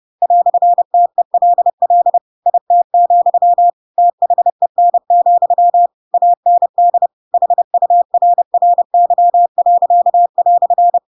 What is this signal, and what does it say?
Morse code, heard as a keyed tone, about 30 wpm, 700 Hz. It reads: "TELL IT, THEN, AND HURRY."
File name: Pinocchio / Pinocchio ch4 30 30 ik1hge